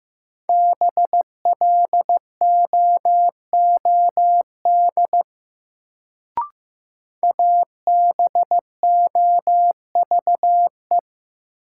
Morse code, heard as a keyed tone, about 15 wpm, 700 Hz. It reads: BLOOD ABOVE
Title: Sentences from Top 1000 Words - Review Part 2 - New Word Drill - 15wpm